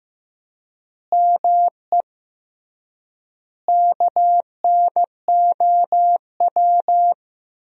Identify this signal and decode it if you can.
Morse code, heard as a keyed tone, about 15 wpm, 700 Hz. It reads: ME KNOW
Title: 3x Word-Spacing - Sentences from Top 300 Words - 15wpm